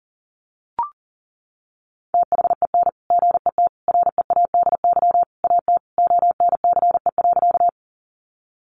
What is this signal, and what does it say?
Morse code, heard as a keyed tone, about 40 wpm, 700 Hz. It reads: THEN GET READY AT ONCE.
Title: Sentences from Top 400 Words 40wpm